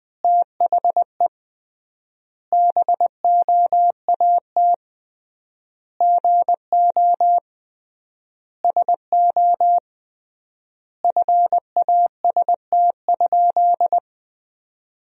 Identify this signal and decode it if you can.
Morse code, heard as a keyed tone, about 21 wpm, 700 Hz. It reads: THE BOAT GO SO FAST?
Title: 3x Word-Spacing - Sentences from Top 500 Words - 20wpm